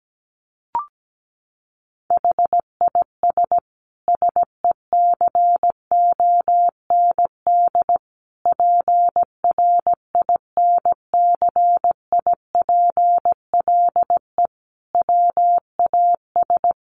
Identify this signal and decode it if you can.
Morse code, heard as a keyed tone, about 17 wpm, 700 Hz. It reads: HIS SECOND PRINCIPLE WAS
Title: Sentences from Top 1400 Words 17wpm